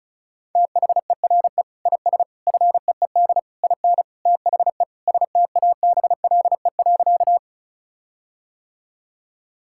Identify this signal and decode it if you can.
Morse code, heard as a keyed tone, about 36 wpm, 700 Hz. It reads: THERE IS FEED IN THE STABLE.
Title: Sentences from Top 1600 Words 35wpm